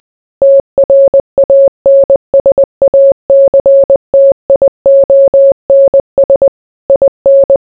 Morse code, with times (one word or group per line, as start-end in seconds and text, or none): 0.4-6.5 TRANSACTIONS
6.9-7.6 IN